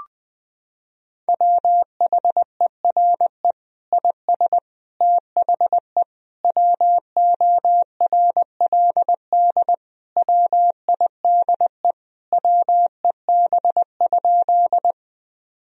Morse code, heard as a keyed tone, about 20 wpm, 700 Hz. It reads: WHERE IS THE WORLD WIDE WEB?
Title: Sentences from Top 1000 Words - Review Part 4 - 20wpm